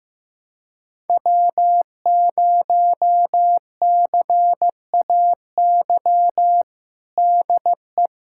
Morse code, heard as a keyed tone, about 15 wpm, 700 Hz. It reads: W0CAY DE
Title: USA Callsign DE Callsign - 5 Characters - 15wpm